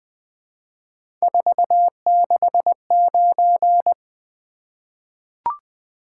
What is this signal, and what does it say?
Morse code, heard as a keyed tone, about 20 wpm, 700 Hz. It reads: 469
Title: Three Letter-Number Combinations - 20wpm